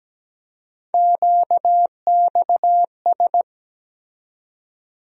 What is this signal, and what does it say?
Morse code, heard as a keyed tone, about 17 wpm, 700 Hz. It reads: QXS
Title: USA Call Sign Postfixes - 3 Letters 17wpm